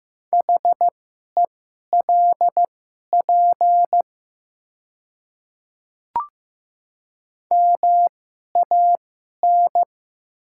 Morse code, heard as a keyed tone, about 15 wpm, spaced as for 11 wpm, 700 Hz. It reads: HELP MAN
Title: 2x Character Spacing - Top 300 Words - 15wpm